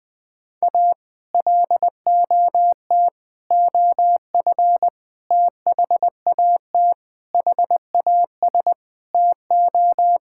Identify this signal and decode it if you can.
Morse code, heard as a keyed tone, about 20 wpm, 700 Hz. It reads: A LOT OF THAT HAS TO
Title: Sentences with 8-Word Phrases - 20wpm